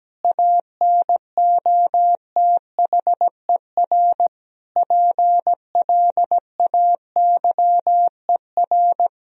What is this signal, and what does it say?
Morse code, heard as a keyed tone, about 17 wpm, 700 Hz. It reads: ANOTHER PLAYER
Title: Sentences from Top 1000 Words - Review Part 4 - 17wpm